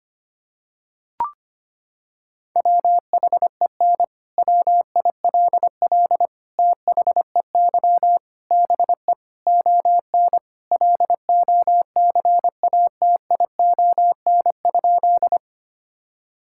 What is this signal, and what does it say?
Morse code, heard as a keyed tone, about 25 wpm, 700 Hz. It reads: WHEN WILL THEY BE ON LOCATION?
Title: Sentences from Top 1000 Words - Review Part 6 - 25wpm